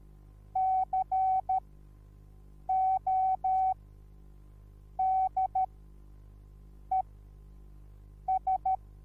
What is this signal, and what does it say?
Morse code, heard as a keyed tone, about 13 wpm, spaced as for 6 wpm, 760 Hz. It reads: CODES
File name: Onlinelehrgang / morsen / DJ4UF Morsen21